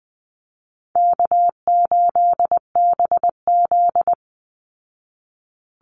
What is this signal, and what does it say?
Morse code, heard as a keyed tone, about 20 wpm, 700 Hz. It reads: K8BZ
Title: Easy Call Signs Repeated 3x - 20wpm